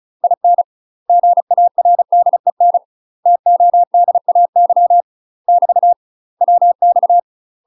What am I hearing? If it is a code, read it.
Morse code, heard as a keyed tone, about 36 wpm, 700 Hz. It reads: IN GARDEN TODAY <BT> WX
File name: UK MASTER QSOs TRAINING / 35WPM / UK MASTER QSOs 0009